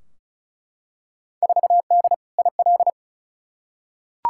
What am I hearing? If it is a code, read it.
Morse code, heard as a keyed tone, about 36 wpm, 700 Hz. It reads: 4D IL
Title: Field Day Exchanges Encore - 35wpm